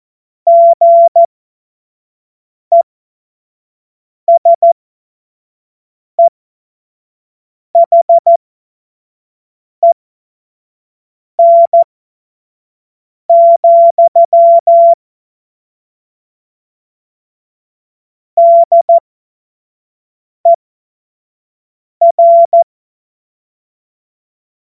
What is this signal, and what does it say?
Morse code, heard as a keyed tone, about 14 wpm, spaced as for 5 wpm, 680 Hz. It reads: GESEHEN, DER
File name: Morsekurs / Witze / Witz 03. Mai 2022 25 F70